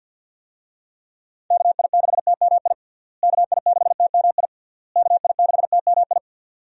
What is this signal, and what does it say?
Morse code, heard as a keyed tone, about 47 wpm, 700 Hz. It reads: KI6TMI KI6TMI KI6TMI
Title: USA Call Signs - 6 Characters - Repeated 3x - 50wpm